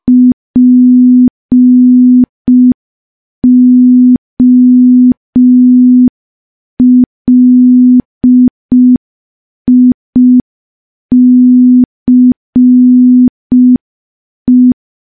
Morse code, heard as a keyed tone, about 5 wpm, 250 Hz. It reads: POLICE
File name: morse / morse-training-05WPM